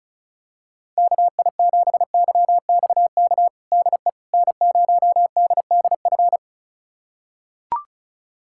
Morse code, heard as a keyed tone, about 34 wpm, 690 Hz. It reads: KI7YXK DE N0DDF
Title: USA Callsign DE Callsign 35wpm